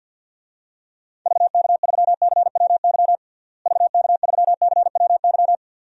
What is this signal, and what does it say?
Morse code, heard as a keyed tone, about 49 wpm, 700 Hz. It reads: VK3CWY VK3CWY
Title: DXCC-100 Call Signs - 6 Characters - Repeated 2x - 50wpm